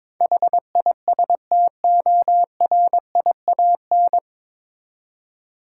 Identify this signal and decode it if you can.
Morse code, heard as a keyed tone, about 22 wpm, 700 Hz. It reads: HISTORIAN
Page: Morse Code Ninja practice 22wpm